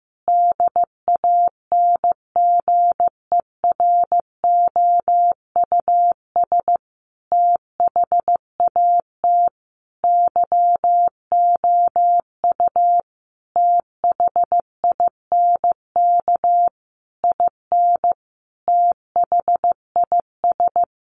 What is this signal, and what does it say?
Morse code, heard as a keyed tone, about 15 wpm, 700 Hz. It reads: DANGEROUS THAT YOU THINK IN THIS
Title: Sentences from Top 1200 Words 15wpm